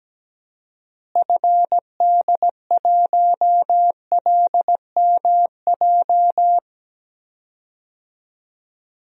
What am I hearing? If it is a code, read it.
Morse code, heard as a keyed tone, about 17 wpm, 700 Hz. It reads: FD1LMJ
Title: DXCC-100 Call Signs - 6 Characters - Repeated 2x - 17wpm